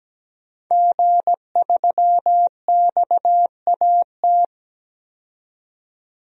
Morse code, heard as a keyed tone, about 17 wpm, 700 Hz. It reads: G3XAT
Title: DXCC-100 Call Signs - 5 Characters - 17wpm